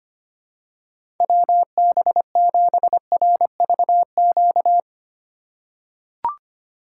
Morse code, heard as a keyed tone, about 25 wpm, 700 Hz. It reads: WB7RVQ
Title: USA Call Signs - 6 Characters - 8x Character Spacing - 25wpm